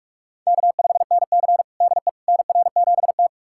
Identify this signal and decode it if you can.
Morse code, heard as a keyed tone, about 41 wpm, 700 Hz. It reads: K5NC DE NR7T
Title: USA Callsign DE Callsign - 4 Characters - 45wpm